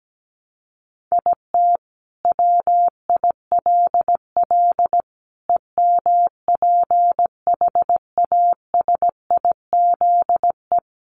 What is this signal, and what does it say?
Morse code, heard as a keyed tone, about 17 wpm, 700 Hz. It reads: IT WILL EMPHASIZE
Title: Sentences from Top 1500 Words 17wpm